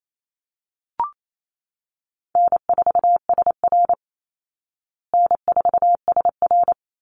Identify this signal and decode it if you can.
Morse code, heard as a keyed tone, about 28 wpm, 700 Hz. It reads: N4SR N4SR
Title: Easy Call Signs Repeated 2x - 28wpm